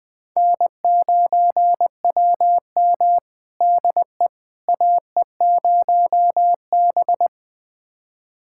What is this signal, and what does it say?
Morse code, heard as a keyed tone, about 21 wpm, 700 Hz. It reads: N9WM DE AE0B
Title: USA Callsign DE Callsign - 4 Characters - 20wpm